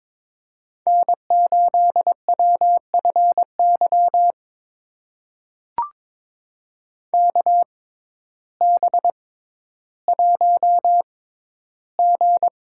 Morse code, heard as a keyed tone, about 22 wpm, spaced as for 12 wpm, 700 Hz. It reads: N8WFY KB1G
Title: USA Call Signs - 6x Character Spacing - 22wpm